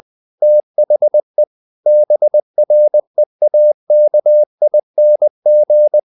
Morse code, heard as a keyed tone, about 20 wpm, 600 Hz. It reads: THE BREAKING
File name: the old man and the sea / the old man and the sea part 1 20wpm